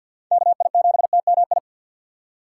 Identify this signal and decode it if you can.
Morse code, heard as a keyed tone, about 47 wpm, 700 Hz. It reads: KI7TMI
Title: USA Call Signs - 6 Characters - 3x Character Spacing - 50wpm